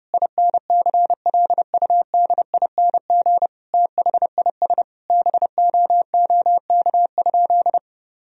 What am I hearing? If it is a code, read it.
Morse code, heard as a keyed tone, about 30 wpm, 700 Hz. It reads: INCLUDING THIS BOOK?
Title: Sentences from Top 1000 Words - Review Part 4 - 30wpm